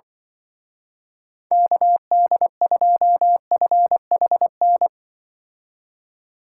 Morse code, heard as a keyed tone, about 24 wpm, 700 Hz. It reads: KD2FHN
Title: Speed Racing - USA Call Signs 20wpm